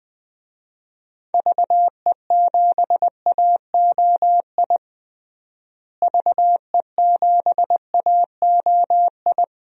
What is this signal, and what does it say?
Morse code, heard as a keyed tone, about 20 wpm, 700 Hz. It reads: VE7AOI VE7AOI
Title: DXCC-100 Call Signs - 6 Characters - Repeated 3x - 20wpm